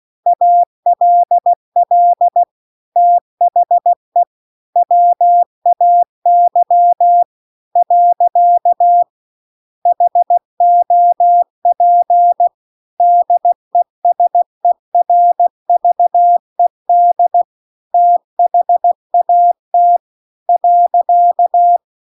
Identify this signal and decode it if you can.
Morse code, heard as a keyed tone, about 16 wpm, 700 Hz. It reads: ALL THE WAY . HOP DESERVED THAT .